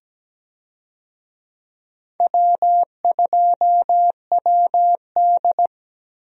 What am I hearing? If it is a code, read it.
Morse code, heard as a keyed tone, about 17 wpm, 700 Hz. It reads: W2WD
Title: USA Call Signs - 4 Characters - Repeated 2x - 17wpm